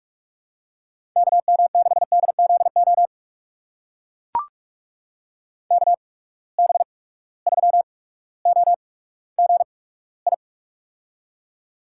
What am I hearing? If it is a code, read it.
Morse code, heard as a keyed tone, about 44 wpm, spaced as for 19 wpm, 700 Hz. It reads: KM6DZO KB3OGI
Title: USA Call Signs - 6 Characters - 8x Character Spacing - 45wpm